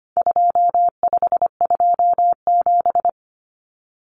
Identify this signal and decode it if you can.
Morse code, heard as a keyed tone, about 25 wpm, 700 Hz. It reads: 2527
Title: Numbers - 4-Digit 25wpm